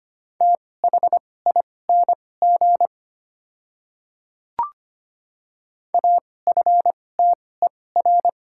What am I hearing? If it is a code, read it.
Morse code, heard as a keyed tone, about 25 wpm, spaced as for 18 wpm, 690 Hz. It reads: THING AFTER
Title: 2x Character Spacing - Top 200 Words - 25wpm